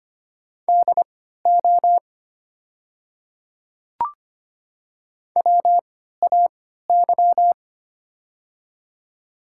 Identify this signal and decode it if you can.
Morse code, heard as a keyed tone, about 25 wpm, spaced as for 14 wpm, 700 Hz. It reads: DO WAY
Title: 3x Character Spacing - Top 200 Words - 25wpm